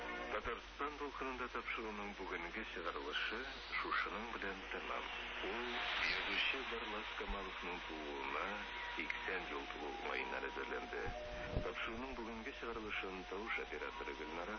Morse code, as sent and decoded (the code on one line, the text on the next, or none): none